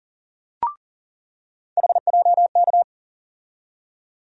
Morse code, heard as a keyed tone, about 40 wpm, 700 Hz. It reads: HJK
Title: USA Call Sign Postfixes - 3 Letters 40wpm